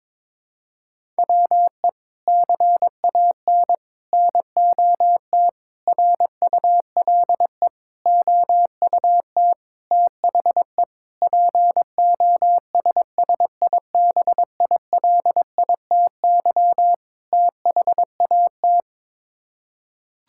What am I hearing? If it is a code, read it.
Morse code, heard as a keyed tone, about 22 wpm, 700 Hz. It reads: WE CAN NOT RULE OUT THE POSSIBILITY THAT
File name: practice / 22wpm / Sets of 8 Words - 22wpm